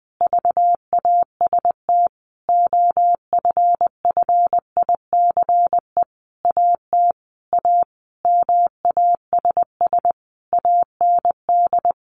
Morse code, heard as a keyed tone, about 20 wpm, 700 Hz. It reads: VAST OFFICE AT A MASS AND
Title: A Sound Of Thunder 20wpm